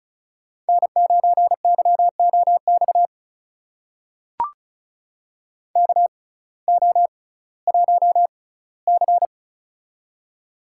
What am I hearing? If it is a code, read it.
Morse code, heard as a keyed tone, about 36 wpm, spaced as for 19 wpm, 700 Hz. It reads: N9YOX KO1C
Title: USA Call Signs - 6x Character Spacing - 35wpm